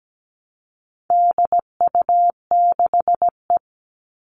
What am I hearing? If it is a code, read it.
Morse code, heard as a keyed tone, about 17 wpm, 700 Hz. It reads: DU6E
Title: All Cut Numbers - 4-Digit 17wpm